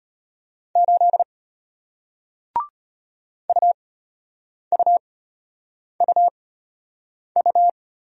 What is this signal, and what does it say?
Morse code, heard as a keyed tone, about 34 wpm, spaced as for 8 wpm, 700 Hz. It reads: 8 UUUU